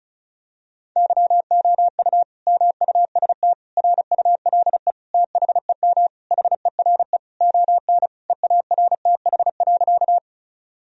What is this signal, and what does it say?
Morse code, heard as a keyed tone, about 36 wpm, 700 Hz. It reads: YOU MUST RULE THEM HERE ON EARTH.
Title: Sentences from Top 1000 Words - Review Part 2 - 35wpm